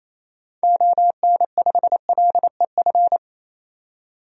Morse code, heard as a keyed tone, about 27 wpm, 700 Hz. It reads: ON5LEF
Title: DXCC-100 Call Signs - 6 Characters - 28wpm